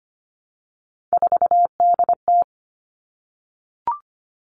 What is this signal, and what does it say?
Morse code, heard as a keyed tone, about 25 wpm, 700 Hz. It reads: 4DT
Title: Three Challenging Letter-Number Combinations 25wpm